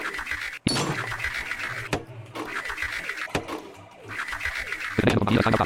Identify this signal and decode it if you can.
Emergency Alert System (SAME) tones, heard as harsh buzzing bursts. It is heard from 1.0 to 4.9 s.